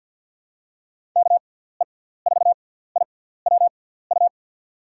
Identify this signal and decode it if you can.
Morse code, heard as a keyed tone, about 48 wpm, spaced as for 17 wpm, 700 Hz. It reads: KE4IWU
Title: USA Call Signs - 6 Characters - 6x Character Spacing - 50wpm